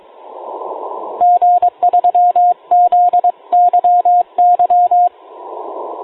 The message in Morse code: G3ZYY